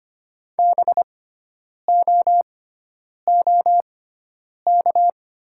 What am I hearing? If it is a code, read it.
Morse code, heard as a keyed tone, about 25 wpm, spaced as for 9 wpm, 700 Hz. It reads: BOOK
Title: 6x Character Spacing - Top 500 Words - 25wpm